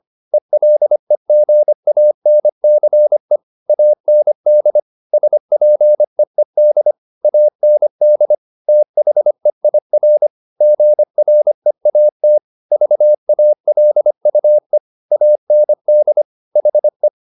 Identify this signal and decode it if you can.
Morse code, heard as a keyed tone, about 25 wpm, 600 Hz. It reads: ELEGANCE AND SPEED AND THEIR GREAT VALUE AND HE
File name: the old man and the sea part 1 25wpm